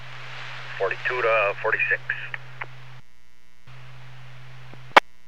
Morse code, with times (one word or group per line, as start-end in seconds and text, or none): none